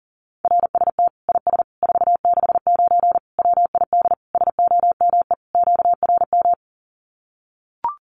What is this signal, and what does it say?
Morse code, heard as a keyed tone, about 40 wpm, 700 Hz. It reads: RST IS 469 WID SOME QRM
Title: Common QSO Phrases 40wpm